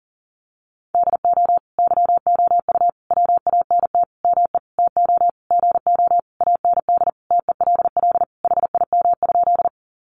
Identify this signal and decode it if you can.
Morse code, heard as a keyed tone, about 40 wpm, 700 Hz. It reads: DO YOU WANT ME TO GO AND TELL HIM?
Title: Sentences from Top 200 Words 40wpm